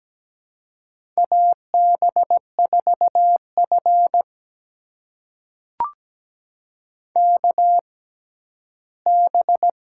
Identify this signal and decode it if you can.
Morse code, heard as a keyed tone, about 17 wpm, spaced as for 11 wpm, 700 Hz. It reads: AB4F KB
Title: USA Call Signs - 6x Character Spacing - 17wpm